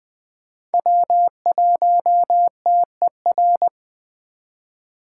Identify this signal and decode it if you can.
Morse code, heard as a keyed tone, about 20 wpm, 700 Hz. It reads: W1TER
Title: K1USN SST QSO Practice - 20wpm